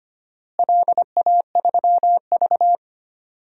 Morse code, heard as a keyed tone, about 25 wpm, 700 Hz. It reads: LA3V
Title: DXCC-100 Call Signs - 4 Characters - Repeated 2x - 25wpm